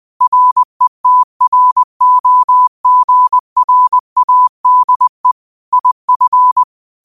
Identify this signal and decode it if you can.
Morse code, heard as a keyed tone, about 20 wpm, 1000 Hz. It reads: RETROGRADE IF
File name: qotdfiles / 20wpm